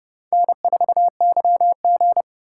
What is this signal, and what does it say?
Morse code, heard as a keyed tone, about 30 wpm, 700 Hz. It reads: N4YG
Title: USA Call Signs - 4x Character Spacing - 30wpm